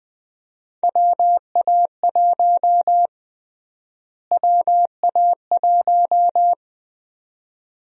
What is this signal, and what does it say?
Morse code, heard as a keyed tone, about 20 wpm, 700 Hz. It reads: WA1 WA1
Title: USA Call Sign Prefixes - Repeated 2x - 20wpm